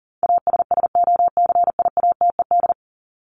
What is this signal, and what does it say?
Morse code, heard as a keyed tone, about 40 wpm, 700 Hz. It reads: ASSOCIATED
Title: ED Suffix - Top 100 Words with Base Words 40wpm